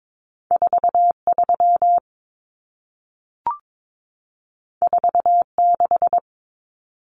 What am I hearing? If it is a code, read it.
Morse code, heard as a keyed tone, about 22 wpm, 700 Hz. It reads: 43 46
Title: Common Cut Numbers - 2-Digit 22wpm